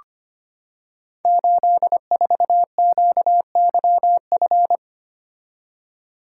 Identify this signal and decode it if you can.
Morse code, heard as a keyed tone, about 25 wpm, 700 Hz. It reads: 84QYF
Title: Code Groups - 5 Letter:Numbers - 25wpm